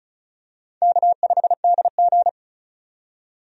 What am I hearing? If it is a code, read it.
Morse code, heard as a keyed tone, about 34 wpm, 700 Hz. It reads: K5DG
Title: USA Call Signs - 4 Characters - 35wpm